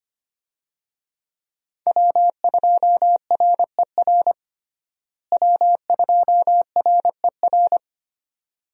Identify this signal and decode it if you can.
Morse code, heard as a keyed tone, about 25 wpm, 700 Hz. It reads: W2RER W2RER
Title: USA Call Signs - 5 Characters - Repeated 2x - 25wpm